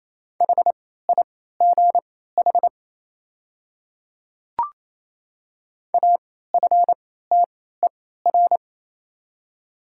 Morse code, heard as a keyed tone, about 27 wpm, spaced as for 16 wpm, 690 Hz. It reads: HIGH AFTER
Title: 3x Character Spacing - Top 200 Words - 28wpm